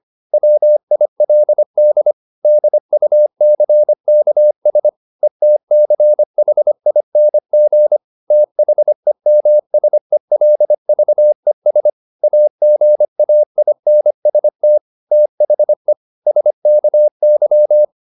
Morse code, heard as a keyed tone, about 25 wpm, 600 Hz. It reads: WILD DUCKS ETCHING THEMSELVES AGAINST THE SKY
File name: the old man and the sea part 1 25wpm